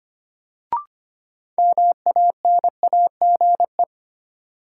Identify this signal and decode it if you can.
Morse code, heard as a keyed tone, about 25 wpm, 700 Hz. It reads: MANAGE
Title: Sentences from Top 1400 Words - New Word Drill Alt - 25wpm